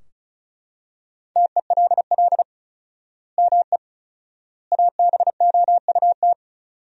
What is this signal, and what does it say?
Morse code, heard as a keyed tone, about 36 wpm, 700 Hz. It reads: TELL ME ABOUT
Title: 4x Word-Spacing - Sentences from Top 200 Words - 35wpm